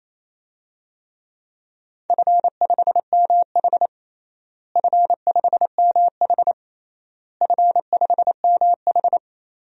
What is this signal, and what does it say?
Morse code, heard as a keyed tone, about 28 wpm, 700 Hz. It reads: F5MH F5MH F5MH
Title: DXCC-100 Call Signs - 4 Characters - Repeated 3x - 28wpm